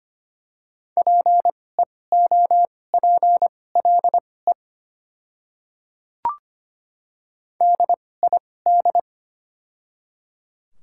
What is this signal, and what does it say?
Morse code, heard as a keyed tone, about 25 wpm, spaced as for 18 wpm, 700 Hz. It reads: PEOPLE DID
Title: 2x Character Spacing - Top 100 Words - 25wpm